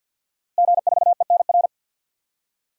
Morse code, heard as a keyed tone, about 52 wpm, 700 Hz. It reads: G4ENR
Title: DXCC-100 Call Signs - 5 Characters - Repeated 3x - 50wpm